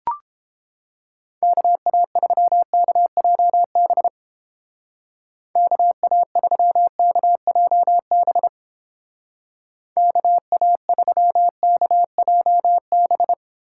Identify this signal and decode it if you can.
Morse code, heard as a keyed tone, about 31 wpm, 700 Hz. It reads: KA3KJB KA3KJB KA3KJB